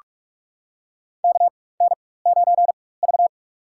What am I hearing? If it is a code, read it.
Morse code, heard as a keyed tone, about 46 wpm, spaced as for 21 wpm, 700 Hz. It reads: KN9V
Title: USA Call Signs - 4 Characters - 4x Character Spacing - 45wpm